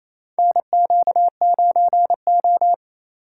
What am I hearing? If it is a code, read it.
Morse code, heard as a keyed tone, about 28 wpm, 700 Hz. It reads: NQ9O
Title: USA Call Signs - 4 Characters - 3x Character Spacing - 28wpm